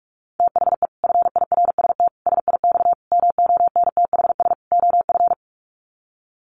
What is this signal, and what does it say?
Morse code, heard as a keyed tone, about 44 wpm, 700 Hz. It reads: THE FIRST SIX MONTHS OF
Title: Sets of 5 Words Encore2 45wpm